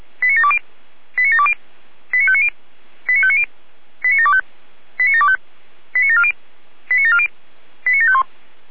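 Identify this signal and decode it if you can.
Five-tone selective call (ZVEI), heard as quick runs of five tones, tones of about 70 ms. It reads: 89720 89720 89590 89590 89725 89725 89740 89740 89751
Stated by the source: ZVEI five-tone selective call 89730 received by SDR.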